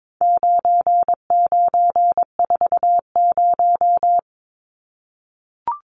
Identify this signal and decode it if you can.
Morse code, heard as a keyed tone, about 22 wpm, 700 Hz. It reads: 9940
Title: Numbers - 4-Digit 22wpm